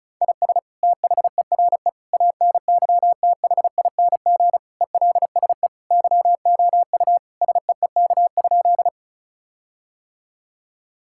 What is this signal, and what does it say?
Morse code, heard as a keyed tone, about 36 wpm, 700 Hz. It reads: IS THERE ANYTHING ELSE YOU SEEK?
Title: Sentences from Top 1600 Words 35wpm